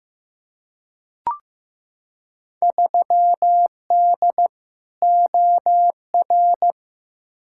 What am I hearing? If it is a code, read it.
Morse code, heard as a keyed tone, about 15 wpm, 700 Hz. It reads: 3D OR
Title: Field Day Exchanges Encore - 15wpm